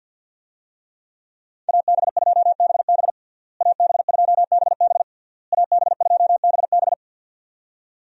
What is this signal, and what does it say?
Morse code, heard as a keyed tone, about 53 wpm, 700 Hz. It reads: AB2BB AB2BB AB2BB
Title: USA Call Signs - 5 Characters - Repeated 3x - 50wpm